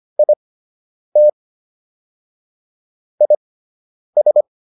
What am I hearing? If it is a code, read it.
Morse code, heard as a keyed tone, about 25 wpm, spaced as for 9 wpm, 600 Hz. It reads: IT IS